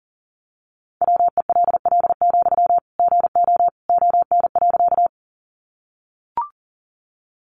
Morse code, heard as a keyed tone, about 40 wpm, 700 Hz. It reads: WELL, GO ON.
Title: Sentences from Top 200 Words 40wpm